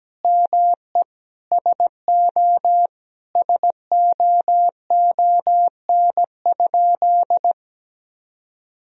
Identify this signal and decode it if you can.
Morse code, heard as a keyed tone, about 17 wpm, 700 Hz. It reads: ME SO SOON?
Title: Sentences from Top 400 Words and Abbreviations 17wpm